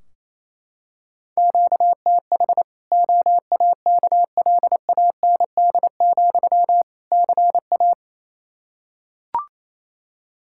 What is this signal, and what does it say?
Morse code, heard as a keyed tone, about 27 wpm, 700 Hz. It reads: QTH OAKLAND, CA
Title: K1USN SST QSO Practice 28wpm